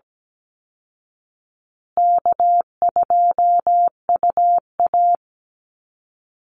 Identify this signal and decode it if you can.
Morse code, heard as a keyed tone, about 17 wpm, 700 Hz. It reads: K2UA
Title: Call Signs Repeated 2x - 17wpm